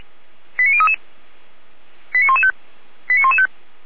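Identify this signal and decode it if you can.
Five-tone selective call (ZVEI), heard as quick runs of five tones, tones of about 70 ms. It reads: 89033 89115 89115
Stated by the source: ZVEI five-tone call 89117 received on an SDR.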